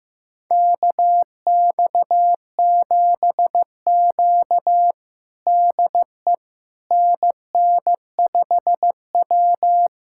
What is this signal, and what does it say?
Morse code, heard as a keyed tone, about 16 wpm, 700 Hz. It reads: KX7Q DE NN5W
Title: USA Callsign DE Callsign - 4 Characters - 15wpm